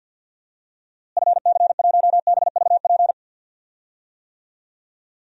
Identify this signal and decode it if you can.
Morse code, heard as a keyed tone, about 53 wpm, 700 Hz. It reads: FC1BVP
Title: DXCC-100 Call Signs - 6 Characters - Repeated 3x - 50wpm